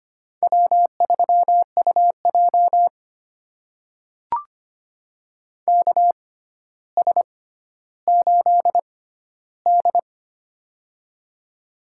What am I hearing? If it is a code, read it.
Morse code, heard as a keyed tone, about 25 wpm, spaced as for 13 wpm, 700 Hz. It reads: W3UJ KS8D
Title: USA Call Signs - 4 Characters - 6x Character Spacing - 25wpm